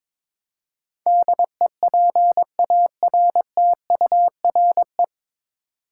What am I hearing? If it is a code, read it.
Morse code, heard as a keyed tone, about 22 wpm, 700 Hz. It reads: DEPARTURE
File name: 9 Letter Words - First 3 Letters Given - 22wpm